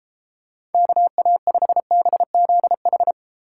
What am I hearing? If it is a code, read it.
Morse code, heard as a keyed tone, about 32 wpm, 700 Hz. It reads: KA5BZH